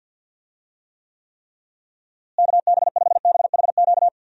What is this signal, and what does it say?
Morse code, heard as a keyed tone, about 50 wpm, 700 Hz. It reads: KB5BHQ
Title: USA Call Signs - 6 Characters - Repeated 3x - 50wpm